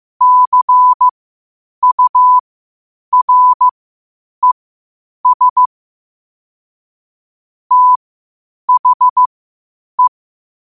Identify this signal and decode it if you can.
Morse code, heard as a keyed tone, about 15 wpm, spaced as for 8 wpm, 1000 Hz. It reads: CURES THE
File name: qotdfiles / 8wpm